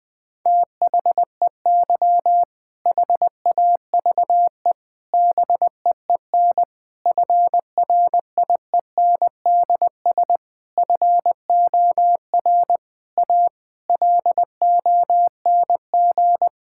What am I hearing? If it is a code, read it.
Morse code, heard as a keyed tone, about 20 wpm, 700 Hz. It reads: THEY HAVE BEEN FRIENDS FOR A LONG
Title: Sentences with 6-Word Phrases - 20wpm